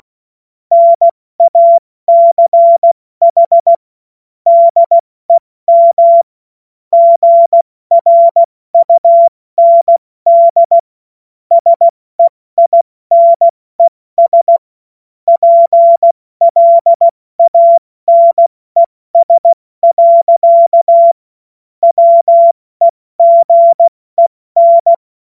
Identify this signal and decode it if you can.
Morse code, heard as a keyed tone, about 16 wpm, 680 Hz. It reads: NACH DEM GRUND SEINES PLANES. WEGEN
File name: Morsekurs / Witze / Witz 19. April 2022 70 F80